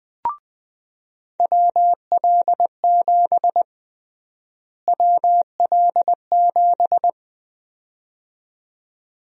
Morse code, heard as a keyed tone, about 20 wpm, 700 Hz. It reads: WL7 WL7
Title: USA Call Sign Prefixes - Repeated 2x - 20wpm